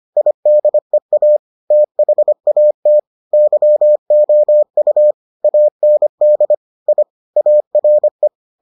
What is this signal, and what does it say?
Morse code, heard as a keyed tone, about 25 wpm, 600 Hz. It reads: IDEA THAT YOU AND I ARE